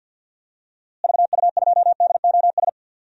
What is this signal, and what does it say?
Morse code, heard as a keyed tone, about 47 wpm, 700 Hz. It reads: VU2DOS
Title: DXCC-100 Call Signs - 6 Characters - Repeated 2x - 50wpm